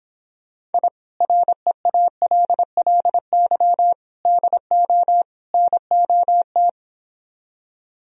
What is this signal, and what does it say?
Morse code, heard as a keyed tone, about 26 wpm, 700 Hz. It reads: I REALLY DO NOT
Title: Speed-Racing - Sets of 4 Words 22wpm